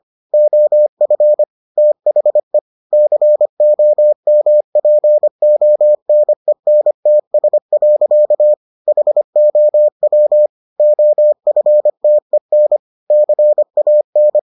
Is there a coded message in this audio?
Yes: Morse code from 0.3 to 14.4 s, heard as a keyed tone, about 25 wpm, 600 Hz. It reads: OF THE COMPONENTS. HOW OFTEN CAN